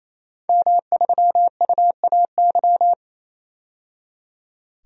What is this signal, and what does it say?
Morse code, heard as a keyed tone, about 28 wpm, 700 Hz. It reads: M3UAY
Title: DXCC-100 Call Signs - 5 Characters - 28wpm